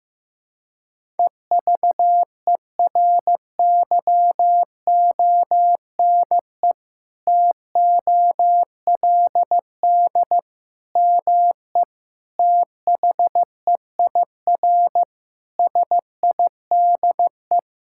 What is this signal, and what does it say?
Morse code, heard as a keyed tone, about 15 wpm, 700 Hz. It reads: EVERYONE TOLD ME THEIR SIDE
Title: Sentences from Top 1000 Words - Review Part 5 - 15wpm